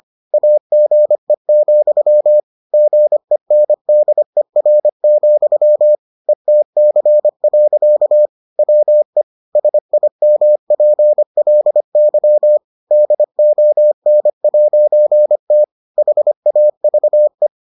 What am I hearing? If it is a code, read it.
Morse code, heard as a keyed tone, about 25 wpm, 600 Hz. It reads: AGE, GENDER, ETC. WE SIMPLY DON'T HAVE